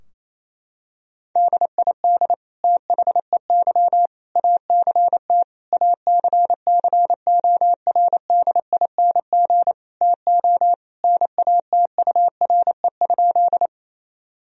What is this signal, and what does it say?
Morse code, heard as a keyed tone, about 27 wpm, 700 Hz. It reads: DID THEY ACT ACCORDING TO NATURE?
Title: Sentences from Top 1000 Words - Review Part 4 - 28wpm